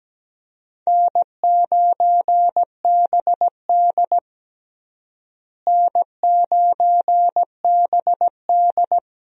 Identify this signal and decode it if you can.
Morse code, heard as a keyed tone, about 17 wpm, 700 Hz. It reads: N9BD N9BD
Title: USA Call Signs - 4 Characters - Repeated 2x - 17wpm